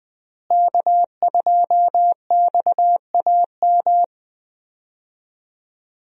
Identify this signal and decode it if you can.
Morse code, heard as a keyed tone, about 21 wpm, 700 Hz. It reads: K2XAM
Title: USA Call Signs - 5 Characters - Repeated 3x - 20wpm